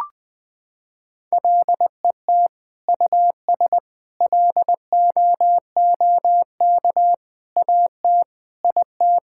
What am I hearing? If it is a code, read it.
Morse code, heard as a keyed tone, about 21 wpm, 700 Hz. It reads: LET US LOOK AT IT
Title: Sentences from Top 1000 Words - Review Part 6 - 20wpm